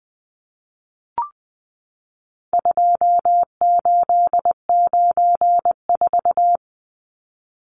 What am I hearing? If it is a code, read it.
Morse code, heard as a keyed tone, about 20 wpm, 700 Hz. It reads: 2894
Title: Numbers - 4-Digit 20wpm